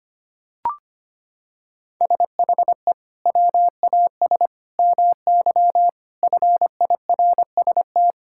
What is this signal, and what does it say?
Morse code, heard as a keyed tone, about 25 wpm, 700 Hz. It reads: SHE WAS MY FIRST
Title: Sentences from Top 1000 Words - Review Part 6 - 25wpm